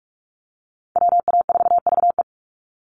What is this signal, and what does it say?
Morse code, heard as a keyed tone, about 44 wpm, 690 Hz. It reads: WA4VE
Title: USA Call Signs 45wpm